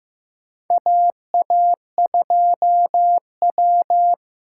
Morse code, heard as a keyed tone, about 15 wpm, 700 Hz. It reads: AA2W
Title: USA Call Signs - 4 Characters - 6x Character Spacing - 15wpm